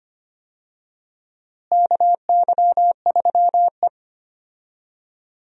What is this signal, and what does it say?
Morse code, heard as a keyed tone, about 25 wpm, 700 Hz. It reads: KY3E
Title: USA Call Signs - 4 Characters - Repeated 2x - 25wpm